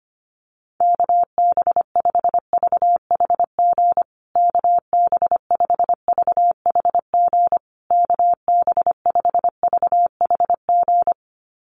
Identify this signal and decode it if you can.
Morse code, heard as a keyed tone, about 25 wpm, 700 Hz. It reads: KB5VHG KB5VHG KB5VHG
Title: USA Call Signs Repeated 3x 25wpm